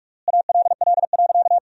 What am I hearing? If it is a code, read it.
Morse code, heard as a keyed tone, about 42 wpm, 700 Hz. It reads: ALL.
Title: 3x Word-Spacing - Sentences from Top 200 Words - 45wpm